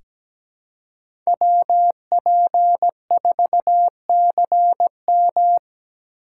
Morse code, heard as a keyed tone, about 17 wpm, 700 Hz. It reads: WP4CM
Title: USA Call Signs - 4x Character Spacing - 17wpm